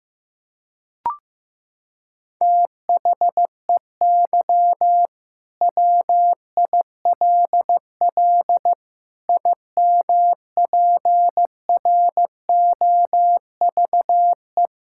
Morse code, heard as a keyed tone, about 15 wpm, 700 Hz. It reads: THEY WILL IMPROVE
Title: Sentences from Top 1000 Words - Review Part 5 - 15wpm